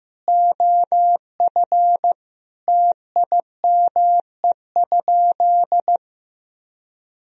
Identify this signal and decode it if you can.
Morse code, heard as a keyed tone, about 15 wpm, 700 Hz. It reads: OF TIME?
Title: Sentences from Top 1000 Words - Review Part 3 - 15wpm